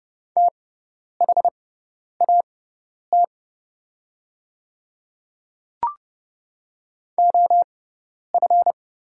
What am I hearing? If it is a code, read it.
Morse code, heard as a keyed tone, about 30 wpm, spaced as for 10 wpm, 700 Hz. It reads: THAT OF
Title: 6x Character Spacing - Top 100 Words - 30wpm